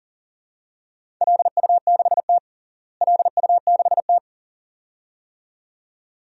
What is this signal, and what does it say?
Morse code, heard as a keyed tone, about 40 wpm, 700 Hz. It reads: LU6T LU6T
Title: DXCC-100 Call Signs - 4 Characters - Repeated 2x - 40wpm